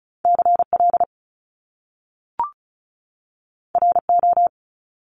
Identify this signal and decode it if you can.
Morse code, heard as a keyed tone, about 36 wpm, 700 Hz. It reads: CL RO
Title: Top Two Letter Combinations in Words 35wpm